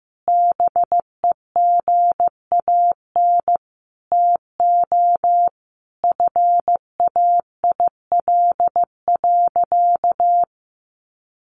Morse code, heard as a keyed tone, about 15 wpm, 700 Hz. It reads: BEGAN TO FAIL.
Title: Sentences from Top 1100 Words 15wpm